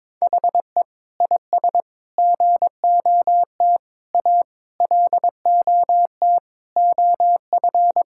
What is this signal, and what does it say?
Morse code, heard as a keyed tone, about 22 wpm, 700 Hz. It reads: HE IS GOT A LOT OF
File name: practice / 22wpm / Sentences with 4-Word Phrases - 22wpm